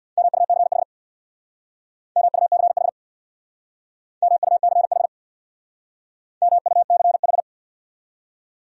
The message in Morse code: MUCH MUCH MUCH MUCH

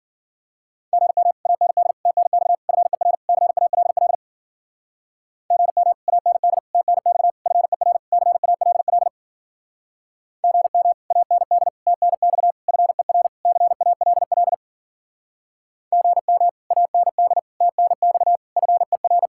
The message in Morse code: GM AND TNX FER CALL GM AND TNX FER CALL GM AND TNX FER CALL GM AND TNX FER

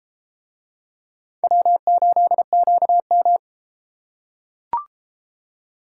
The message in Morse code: W8QM

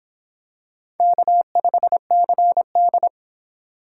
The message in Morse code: K5CD

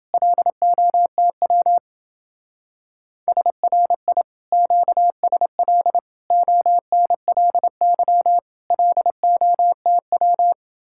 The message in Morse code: LOTW SRI QSL ONLY LOTW

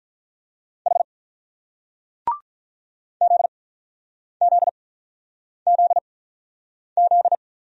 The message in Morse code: H ZZZZ

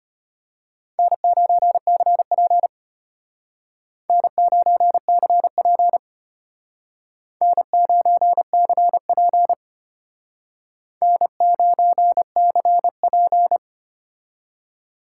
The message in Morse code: N9CP N9CP N9CP N9CP

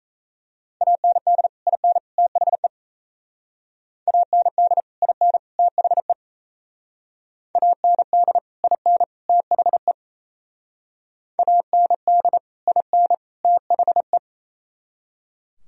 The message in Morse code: AND IN THE AND IN THE AND IN THE AND IN THE

